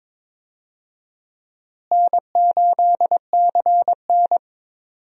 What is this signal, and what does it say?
Morse code, heard as a keyed tone, about 22 wpm, 700 Hz. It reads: N8CN
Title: USA Call Signs - 4 Characters - Repeated 2x - 22wpm